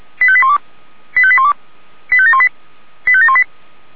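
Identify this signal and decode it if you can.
Five-tone selective call (ZVEI), heard as quick runs of five tones, tones of about 70 ms. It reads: 85812 85812 85718 85718
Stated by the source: Five-tone ZVEI selective call 89155 received by SDR.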